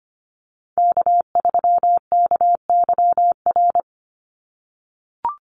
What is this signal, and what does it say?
Morse code, heard as a keyed tone, about 25 wpm, 700 Hz. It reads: K3KYR
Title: NA SOTA Chaser Callsigns 25wpm